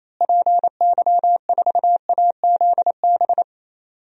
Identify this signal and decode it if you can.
Morse code, heard as a keyed tone, about 27 wpm, 700 Hz. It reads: PY4AZB
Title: DXCC-100 Call Signs - 6 Characters - Repeated 3x - 28wpm